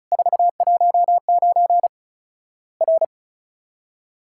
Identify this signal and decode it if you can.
Morse code, heard as a keyed tone, about 36 wpm, 700 Hz. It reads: 419 R
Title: Worked All Europe DX Contest - QTC Exchange - 35wpm